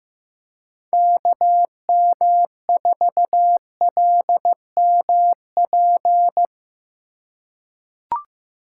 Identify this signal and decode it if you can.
Morse code, heard as a keyed tone, about 15 wpm, 700 Hz. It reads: KM4LMP
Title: USA Call Signs - 6 Characters - 3x Character Spacing - 15wpm